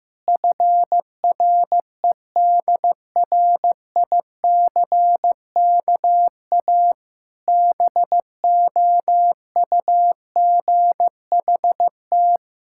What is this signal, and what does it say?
Morse code, heard as a keyed tone, about 15 wpm, 700 Hz. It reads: FREDRICKA BOUGHT
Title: Pangram Challenge v2 - 15wpm